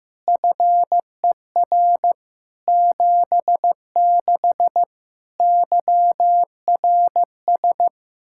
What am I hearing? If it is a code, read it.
Morse code, heard as a keyed tone, about 15 wpm, 700 Hz. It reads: FER 76 YRS
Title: K1USN SST QSO Practice 15wpm